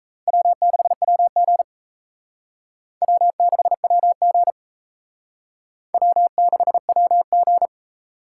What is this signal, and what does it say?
Morse code, heard as a keyed tone, about 34 wpm, 700 Hz. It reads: W6WG W6WG W6WG
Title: Speed Racing - USA 1x2 Call Signs 28wpm